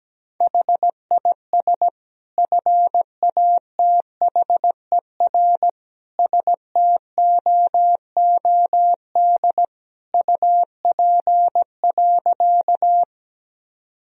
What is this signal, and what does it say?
Morse code, heard as a keyed tone, about 17 wpm, 700 Hz. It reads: HIS FATHER STOOD UP.
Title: Sentences from Top 1000 Words - Review Part 2 - 17wpm